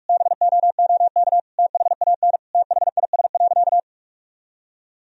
Morse code, heard as a keyed tone, about 46 wpm, 700 Hz. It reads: BOOK THAN THIS.
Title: Sentences from Top 1000 Words - Review Part 3 - 45wpm